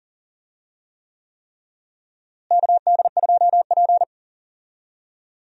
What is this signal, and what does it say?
Morse code, heard as a keyed tone, about 40 wpm, 700 Hz. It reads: KD2P
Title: USA Call Signs - 4 Characters - Repeated 3x - 40wpm